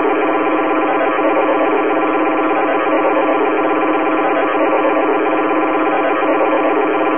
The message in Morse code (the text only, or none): none